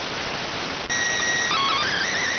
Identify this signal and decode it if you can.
SSTV (slow-scan television), PD 290.